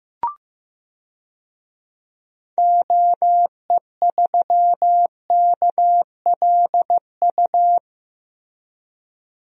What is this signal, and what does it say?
Morse code, heard as a keyed tone, about 15 wpm, 700 Hz. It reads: OE3KLU
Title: DXCC-100 Call Signs - 6 Characters - Repeated 3x - 15wpm